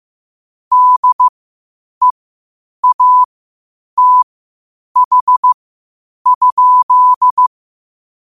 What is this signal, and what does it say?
Morse code, heard as a keyed tone, about 15 wpm, spaced as for 8 wpm, 1000 Hz. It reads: DEATH?
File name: qotdfiles / 8wpm